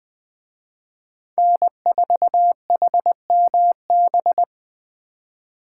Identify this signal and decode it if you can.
Morse code, heard as a keyed tone, about 20 wpm, 700 Hz. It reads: N4HMB